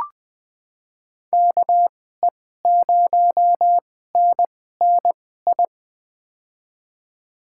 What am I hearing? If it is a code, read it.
Morse code, heard as a keyed tone, about 20 wpm, spaced as for 15 wpm, 700 Hz. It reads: KE0NNI
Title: USA Call Signs - 2x Character Spacing - 20wpm